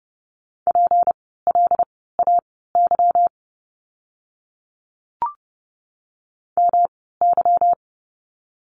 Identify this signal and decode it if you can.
Morse code, heard as a keyed tone, about 30 wpm, spaced as for 17 wpm, 700 Hz. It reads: PLAY MY
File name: course / 30:10wpm / 237 Lesson 38f - Letter q - Words 30wpm